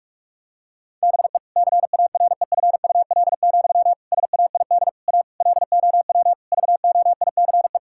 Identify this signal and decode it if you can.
Morse code, heard as a keyed tone, about 41 wpm, 700 Hz. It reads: BE CAREFUL, SAID A LOW VOICE
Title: Sentences from Top 1600 Words 45wpm